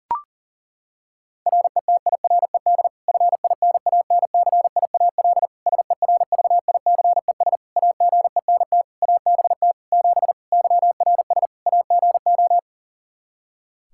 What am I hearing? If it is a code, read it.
Morse code, heard as a keyed tone, about 40 wpm, 700 Hz. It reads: RETIRED FINANCIAL SERVICES AGENT ABT 7 YRS AGO
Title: K1USN SST QSO Practice 40wpm